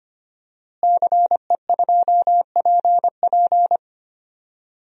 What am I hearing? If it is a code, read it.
Morse code, heard as a keyed tone, about 25 wpm, 700 Hz. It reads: CE2PP
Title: DXCC-100 Call Signs - 5 Characters - Repeated 3x - 25wpm